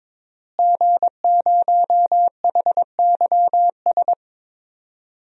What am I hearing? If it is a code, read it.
Morse code, heard as a keyed tone, about 22 wpm, 700 Hz. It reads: G0HYS
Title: DXCC-100 Call Signs - 5 Characters - Repeated 3x - 22wpm